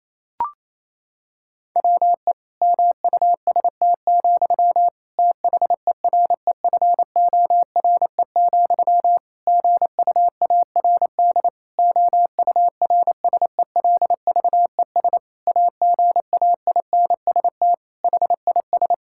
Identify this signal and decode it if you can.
Morse code, heard as a keyed tone, about 28 wpm, 700 Hz. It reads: WE MUST, THEREFORE, GUARD OURSELVES AGAINST HIS
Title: Sentences from Top 1600 Words 28wpm